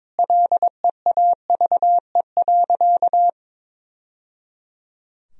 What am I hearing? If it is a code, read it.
Morse code, heard as a keyed tone, about 21 wpm, 700 Hz. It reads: LEAVE.
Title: Sentences from Top 1000 Words - Review Part 4 - 22wpm